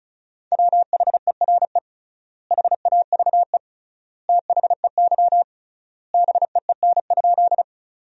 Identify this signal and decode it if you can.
Morse code, heard as a keyed tone, about 36 wpm, 700 Hz. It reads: WHERE HAVE THEY BEEN?
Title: 3x Word-Spacing - Sentences from Top 200 Words - 35wpm